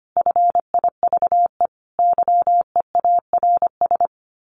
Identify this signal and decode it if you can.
Morse code, heard as a keyed tone, about 25 wpm, 700 Hz. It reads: FIVE YEARS
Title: Sets of 5 Words Encore 25wpm